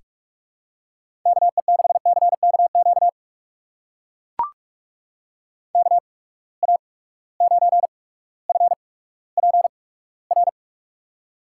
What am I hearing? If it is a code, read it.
Morse code, heard as a keyed tone, about 42 wpm, spaced as for 19 wpm, 700 Hz. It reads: KE6CKQ KA9FPR
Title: USA Call Signs - 6 Characters - 8x Character Spacing - 45wpm